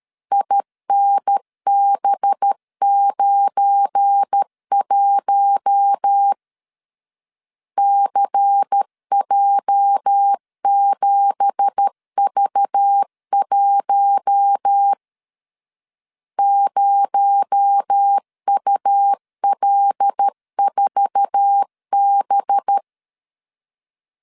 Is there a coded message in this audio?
Yes: Morse code from 0.3 to 22.8 s, heard as a keyed tone, about 13 wpm, 800 Hz. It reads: INB91 CJ7V1 0UL4B